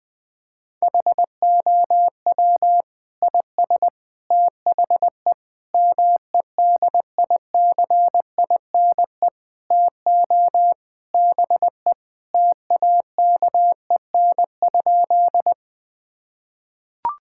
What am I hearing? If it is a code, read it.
Morse code, heard as a keyed tone, about 21 wpm, 700 Hz. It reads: HOW IS THE MEDICINE TO BE TAKEN?
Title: Sentences from Top 1000 Words - Review Part 6 - 20wpm